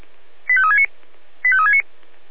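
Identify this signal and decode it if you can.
Five-tone selective call (ZVEI), heard as quick runs of five tones, tones of about 70 ms. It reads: 85369 85369
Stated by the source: ZVEI five-tone call 85360 received by SDR.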